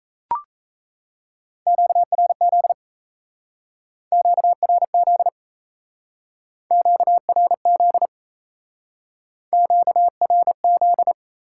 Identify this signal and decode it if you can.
Morse code, heard as a keyed tone, about 36 wpm, 700 Hz. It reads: QRZ QRZ QRZ QRZ